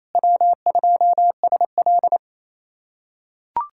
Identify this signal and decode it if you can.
Morse code, heard as a keyed tone, about 28 wpm, 700 Hz. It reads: W2SL
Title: USA Callsign DE Callsign 28wpm